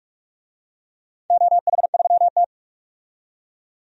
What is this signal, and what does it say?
Morse code, heard as a keyed tone, about 44 wpm, 700 Hz. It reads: OH3T
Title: DXCC-100 Call Signs - 4 Characters - 45wpm